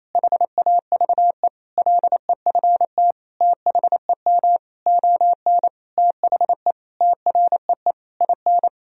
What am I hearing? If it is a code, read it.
Morse code, heard as a keyed tone, about 28 wpm, 700 Hz. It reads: HAVE LEFT THEM ON THE TREE IN
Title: Sentences from Top 1000 Words - Review Part 1 - 28wpm